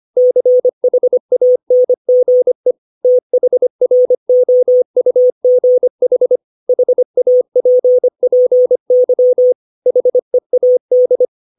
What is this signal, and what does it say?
Morse code, heard as a keyed tone, about 25 wpm, 500 Hz. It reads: CHANGE THROUGH HAPPY HEAD
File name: 632 words / 632 Words 25WPM